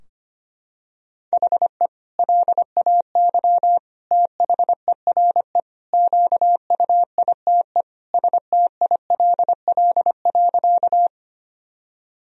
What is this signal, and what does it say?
Morse code, heard as a keyed tone, about 25 wpm, 700 Hz. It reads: HE LAY THERE QUITE STILL.
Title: Sentences from Top 1000 Words - Review Part 2 - 25wpm